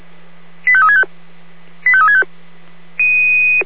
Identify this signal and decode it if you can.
Five-tone selective call (ZVEI), heard as quick runs of five tones, tones of about 70 ms. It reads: 85365 85365